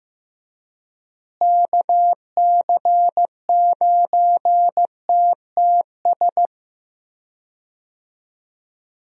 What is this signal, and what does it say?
Morse code, heard as a keyed tone, about 16 wpm, 700 Hz. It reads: KC9TTS
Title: USA Call Signs - 6 Characters - Repeated 3x - 15wpm